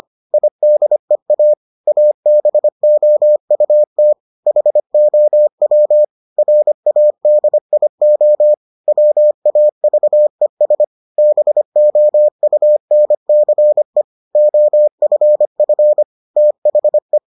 Morse code, heard as a keyed tone, about 25 wpm, 600 Hz. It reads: IDEA ABOUT HOW RADIO WAVES BOUNCE OFF THE